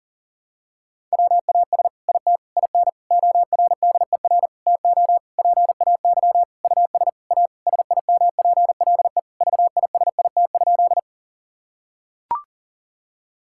Morse code, heard as a keyed tone, about 40 wpm, 700 Hz. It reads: WAS IT IN ORDER TO PAY US A SIMPLE VISIT?
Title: Sentences from Top 1000 Words - Review Part 6 - 40wpm